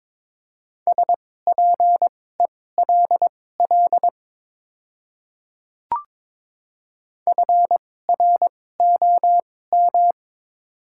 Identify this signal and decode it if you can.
Morse code, heard as a keyed tone, about 22 wpm, spaced as for 16 wpm, 700 Hz. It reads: SPELL FROM
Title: 2x Character Spacing - Top 300 Words - 22wpm